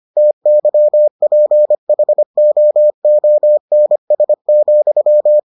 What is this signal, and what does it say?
Morse code, heard as a keyed tone, about 25 wpm, 600 Hz. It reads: TYPHOONS,